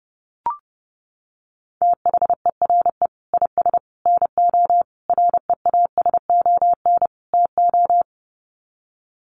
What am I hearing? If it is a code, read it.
Morse code, heard as a keyed tone, about 30 wpm, 700 Hz. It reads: THERE IS NO REASON TO
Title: Sets of 5 Words Encore 30wpm